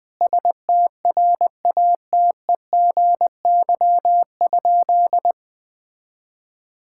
Morse code, heard as a keyed tone, about 20 wpm, 700 Hz. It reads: STRATEGY?
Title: Sentences from Top 1000 Words - Review Part 5 - 20wpm